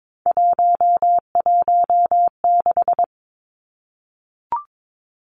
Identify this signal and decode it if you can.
Morse code, heard as a keyed tone, about 22 wpm, 700 Hz. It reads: 116
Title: Numbers - 3-Digit 22wpm